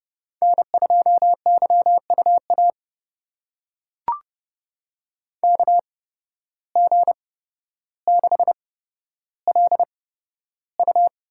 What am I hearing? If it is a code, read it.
Morse code, heard as a keyed tone, about 28 wpm, spaced as for 13 wpm, 700 Hz. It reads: N2YUA KG6LU
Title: USA Call Signs - 8x Character Spacing - 30wpm